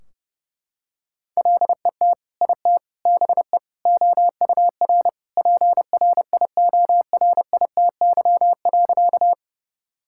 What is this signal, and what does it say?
Morse code, heard as a keyed tone, about 30 wpm, 700 Hz. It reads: LET IT BE OUR PRIORITY.
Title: Sentences from Top 1600 Words 30wpm